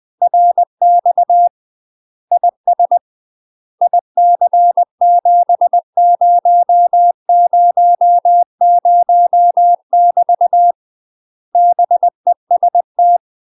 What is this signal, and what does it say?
Morse code, heard as a keyed tone, about 20 wpm, 700 Hz. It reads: RX IS IC7000 <BT> BEST
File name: UK MASTER QSOs TRAINING / 20WPM / UK MASTER QSOs 0025